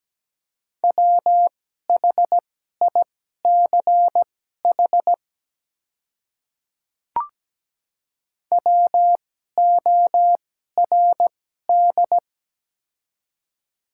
Morse code, heard as a keyed tone, about 17 wpm, spaced as for 12 wpm, 700 Hz. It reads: WHICH WORD
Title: 2x Character Spacing - Top 100 Words - 17wpm